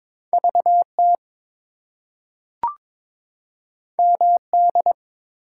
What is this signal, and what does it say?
Morse code, heard as a keyed tone, about 22 wpm, 700 Hz. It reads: VT MD
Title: US States and Canadian Provinces 22wpm